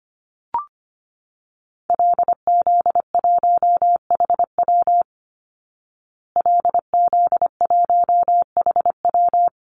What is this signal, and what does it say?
Morse code, heard as a keyed tone, about 25 wpm, 700 Hz. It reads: LZ1HW LZ1HW
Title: Call Signs Repeated 2x - 25wpm